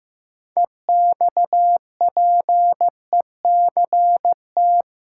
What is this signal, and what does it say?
Morse code, heard as a keyed tone, about 15 wpm, 700 Hz. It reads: EXPECT
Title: Sentences from Top 1100 Words - New Word Drill Alt - 15wpm